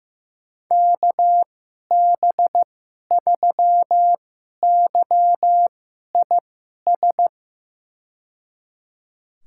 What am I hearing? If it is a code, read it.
Morse code, heard as a keyed tone, about 15 wpm, spaced as for 11 wpm, 700 Hz. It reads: KB3YIS
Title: USA Call Signs - 6 Characters - 2x Character Spacing - 15wpm